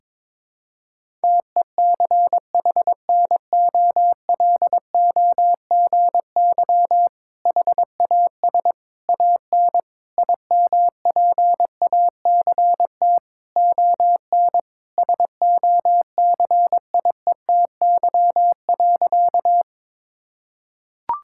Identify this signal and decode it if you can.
Morse code, heard as a keyed tone, about 22 wpm, 700 Hz. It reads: TECHNOLOGY HAS AN IMPACT ON SOCIETY.
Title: Sentences from Top 1000 Words - Review Part 6 - 22wpm